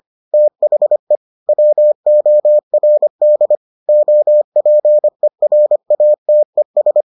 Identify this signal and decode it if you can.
Morse code, heard as a keyed tone, about 25 wpm, 600 Hz. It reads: THE WORD OPERATES